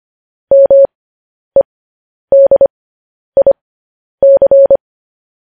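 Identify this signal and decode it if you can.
Morse code, heard as a keyed tone, about 25 wpm, spaced as for 10 wpm, 550 Hz. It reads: MEDIC
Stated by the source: Morse code practice sending news headlines at 10 wpm.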